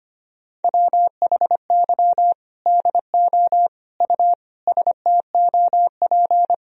Morse code, heard as a keyed tone, about 25 wpm, 700 Hz. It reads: WHY DO U STOP
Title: Sentences from Top 300 Words and Abbreviations 25wpm